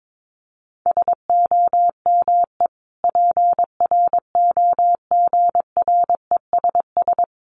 Morse code, heard as a keyed tone, about 22 wpm, 700 Hz. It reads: SOME PROGRESS
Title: Sentences from Top 1400 Words 22wpm